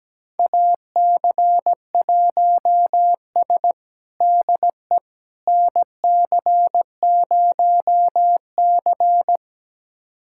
Morse code, heard as a keyed tone, about 17 wpm, 700 Hz. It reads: AC1S DE NC0C
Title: USA Callsign DE Callsign - 4 Characters - 17wpm